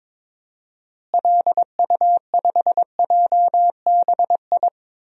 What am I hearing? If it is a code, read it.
Morse code, heard as a keyed tone, about 22 wpm, 700 Hz. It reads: LU5JBI
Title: DXCC-100 Call Signs - 6 Characters - Repeated 3x - 22wpm